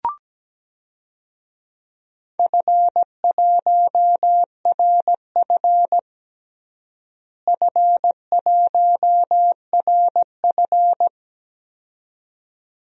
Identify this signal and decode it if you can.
Morse code, heard as a keyed tone, about 17 wpm, 700 Hz. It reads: F1RF F1RF
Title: DXCC-100 Call Signs - 4 Characters - Repeated 2x - 17wpm